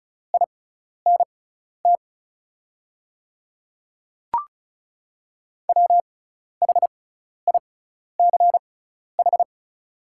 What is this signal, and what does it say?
Morse code, heard as a keyed tone, about 36 wpm, spaced as for 12 wpm, 700 Hz. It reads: INT WHICH
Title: 6x Character Spacing - Top 500 Words - 35wpm